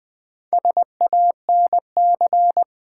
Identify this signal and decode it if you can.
Morse code, heard as a keyed tone, about 21 wpm, 700 Hz. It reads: SANC
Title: Build A Word - 2 Letters At A Time - 8-Letter Words - 20wpm